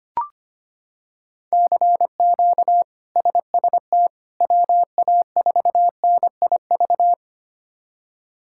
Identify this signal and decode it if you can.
Morse code, heard as a keyed tone, about 25 wpm, 700 Hz. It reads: CQ SST WA4NIV
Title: K1USN SST QSO Practice - 25wpm